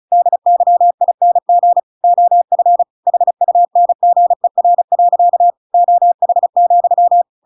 Morse code, heard as a keyed tone, about 34 wpm, 700 Hz. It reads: DYING OF HUNGER. OH,